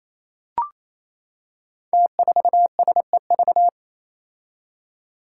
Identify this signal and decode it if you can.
Morse code, heard as a keyed tone, about 28 wpm, 700 Hz. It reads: T4SEV
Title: Five Challenging Letter-Number Combinations 28wpm